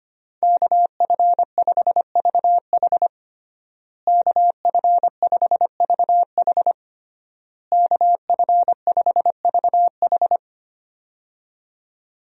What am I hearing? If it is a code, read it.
Morse code, heard as a keyed tone, about 25 wpm, 700 Hz. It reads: KF5VH KF5VH KF5VH
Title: USA Call Signs - 5 Characters - Repeated 3x - 25wpm